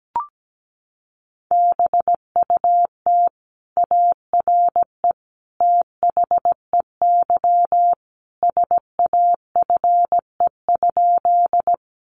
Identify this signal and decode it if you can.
Morse code, heard as a keyed tone, about 17 wpm, 700 Hz. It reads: BUT ARE THEY SAFE?
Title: Sentences from Top 900 Words 17wpm